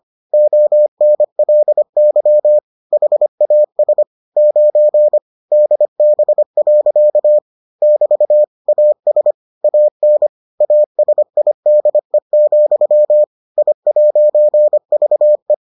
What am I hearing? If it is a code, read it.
Morse code, heard as a keyed tone, about 25 wpm, 600 Hz. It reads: ONLY HAS 9 DB. <BT> AS AN ASIDE, I'VE